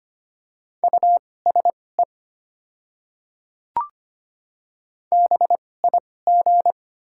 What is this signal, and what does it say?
Morse code, heard as a keyed tone, about 25 wpm, spaced as for 18 wpm, 700 Hz. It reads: USE BIG
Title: 2x Character Spacing - Top 200 Words - 25wpm